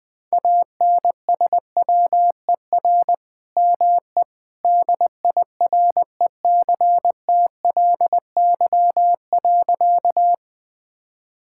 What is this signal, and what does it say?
Morse code, heard as a keyed tone, about 21 wpm, 700 Hz. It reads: ANSWER ME DIRECTLY.
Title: Sentences from Top 1000 Words - Review Part 5 - 20wpm